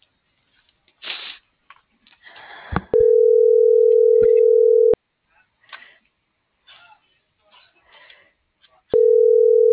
Ringback tone (North America), 440 + 480 Hz.